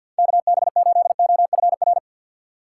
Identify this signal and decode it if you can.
Morse code, heard as a keyed tone, about 47 wpm, 700 Hz. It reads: KB8OFR
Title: USA Call Signs - 6 Characters - 3x Character Spacing - 50wpm